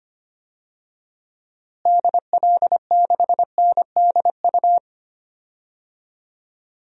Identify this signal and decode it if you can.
Morse code, heard as a keyed tone, about 25 wpm, 700 Hz. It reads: DL6NDU
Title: DXCC-100 Call Signs - 6 Characters - Repeated 2x - 25wpm